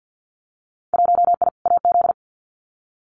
Morse code, heard as a keyed tone, about 50 wpm, 700 Hz. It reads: 1I AZ
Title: Winter Field Day Exchanges 50wpm